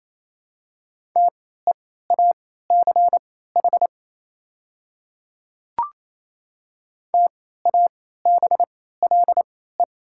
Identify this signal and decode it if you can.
Morse code, heard as a keyed tone, about 28 wpm, spaced as for 16 wpm, 700 Hz. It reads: TEACH TABLE
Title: 3x Character Spacing - Top 500 Words - 28wpm